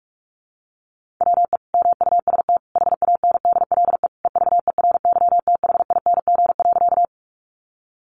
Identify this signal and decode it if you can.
Morse code, heard as a keyed tone, about 44 wpm, 700 Hz. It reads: WE MUST HANDLE EVERYTHING.
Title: Sentences from Top 1100 Words 45wpm